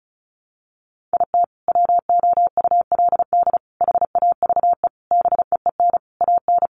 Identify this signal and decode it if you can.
Morse code, heard as a keyed tone, about 34 wpm, 700 Hz. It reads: IT WOULD HAVE BEEN AN
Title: Sentences from Top 1100 Words 35wpm